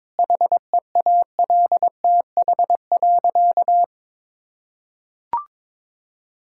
Morse code, heard as a keyed tone, about 23 wpm, 700 Hz. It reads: HEALTH.
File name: practice / 22wpm / Easy Sentences with Hints - 22wpm